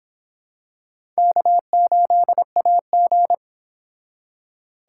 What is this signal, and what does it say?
Morse code, heard as a keyed tone, about 26 wpm, 700 Hz. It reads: K8AG